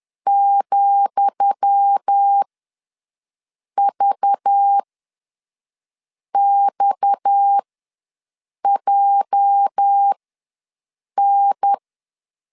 Morse code, heard as a keyed tone, about 10 wpm, spaced as for 5 wpm, 800 Hz. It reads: ,VXJN
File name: LEZIONI / 10 SIGNS / 10D COMA ws 10wpm 4314